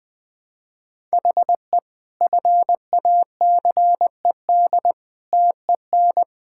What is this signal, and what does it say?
Morse code, heard as a keyed tone, about 20 wpm, 700 Hz. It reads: HE FACED TEN
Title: Sentences with 9-Word Phrases - 20wpm